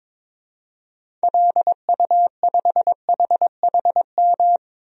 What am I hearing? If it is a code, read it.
Morse code, heard as a keyed tone, about 22 wpm, 700 Hz. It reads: LU5HHM
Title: DXCC-100 Call Signs - 6 Characters - 22wpm